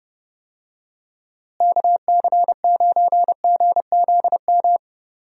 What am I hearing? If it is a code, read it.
Morse code, heard as a keyed tone, about 30 wpm, 700 Hz. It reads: KC9GZM
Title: USA Call Signs - 6 Characters - Repeated 3x - 30wpm